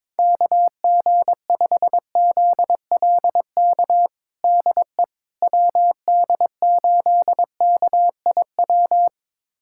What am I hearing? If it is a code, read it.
Morse code, heard as a keyed tone, about 22 wpm, 700 Hz. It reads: KG5ZLK DE WD8KIW
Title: USA Callsign DE Callsign - 6 Characters - 22wpm